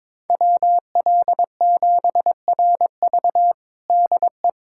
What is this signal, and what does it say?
Morse code, heard as a keyed tone, about 22 wpm, 700 Hz. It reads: WL7RV DE
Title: USA Callsign DE Callsign - 5 Characters - 22wpm